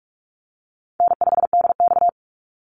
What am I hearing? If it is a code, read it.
Morse code, heard as a keyed tone, about 44 wpm, 700 Hz. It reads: N5DX
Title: Easy Call Signs Repeated 3x - 45wpm